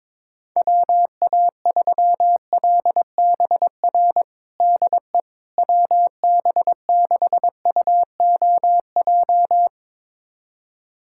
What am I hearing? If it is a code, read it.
Morse code, heard as a keyed tone, about 22 wpm, 700 Hz. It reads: WA3LBR DE WB6UOJ
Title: USA Callsign DE Callsign - 6 Characters - 22wpm